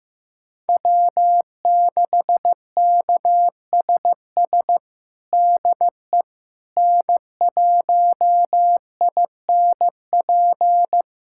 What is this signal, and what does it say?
Morse code, heard as a keyed tone, about 15 wpm, 700 Hz. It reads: W6KSS DE N1INP
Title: USA Callsign DE Callsign - 5 Characters - 15wpm